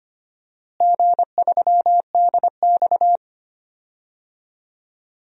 Morse code, heard as a keyed tone, about 25 wpm, 700 Hz. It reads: G3DX
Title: DXCC-100 Call Signs - 4 Characters - Repeated 2x - 25wpm